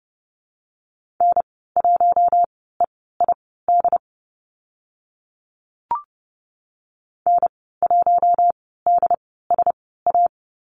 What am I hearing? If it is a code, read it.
Morse code, heard as a keyed tone, about 30 wpm, spaced as for 17 wpm, 700 Hz. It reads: N1EID N1DSA
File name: Lesson 15g - Number 5 - Call Signs 30wpm